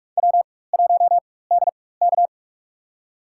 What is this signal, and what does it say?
Morse code, heard as a keyed tone, about 46 wpm, spaced as for 21 wpm, 700 Hz. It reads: W1DK
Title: USA Call Signs - 4 Characters - 4x Character Spacing - 45wpm